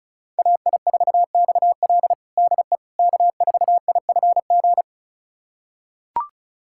Morse code, heard as a keyed tone, about 36 wpm, 700 Hz. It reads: AI4XL DE K4IFG
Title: USA Callsign DE Callsign - 5 Characters - 35wpm